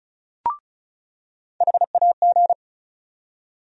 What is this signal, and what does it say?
Morse code, heard as a keyed tone, about 36 wpm, 700 Hz. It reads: HAG
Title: USA Call Sign Postfixes - 3 Letters 35wpm